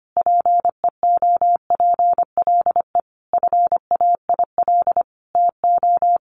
Morse code, heard as a keyed tone, about 25 wpm, 700 Hz. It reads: PEOPLE FAIL TO
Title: Sentences from Top 1100 Words 25wpm